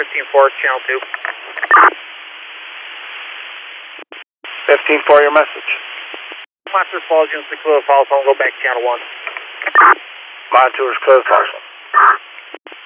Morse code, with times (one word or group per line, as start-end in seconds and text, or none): none